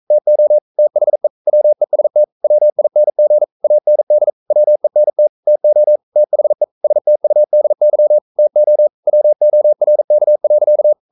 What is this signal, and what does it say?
Morse code, heard as a keyed tone, about 42 wpm, 600 Hz. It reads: TO THE WEST WING AND WENT TO THE STUDY TO WORK.
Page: Romanen Never af Ken Follet på engelsk.